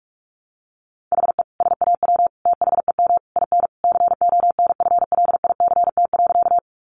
Morse code, heard as a keyed tone, about 44 wpm, 700 Hz. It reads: HE SAW THEM IN CONFLICT.
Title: Sentences from Top 1300 Words 45wpm